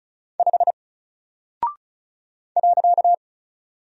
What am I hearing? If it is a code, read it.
Morse code, heard as a keyed tone, about 36 wpm, 700 Hz. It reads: 5 .
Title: Single Character (Letter:Number:Punctuation:Prosign) Alt 35wpm